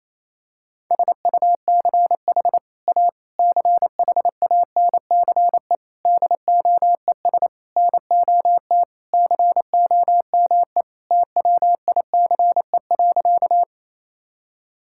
Morse code, heard as a keyed tone, about 28 wpm, 700 Hz. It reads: SUCH A CHANCE DOES NOT COME TWICE.
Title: Sentences from Top 1600 Words 28wpm